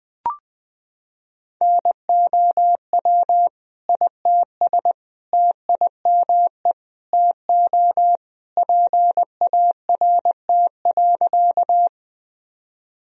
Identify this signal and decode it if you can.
Morse code, heard as a keyed tone, about 20 wpm, 700 Hz. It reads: NOW ITS TIME TO PART.
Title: Sentences from Top 1000 Words - Review Part 1 - 20wpm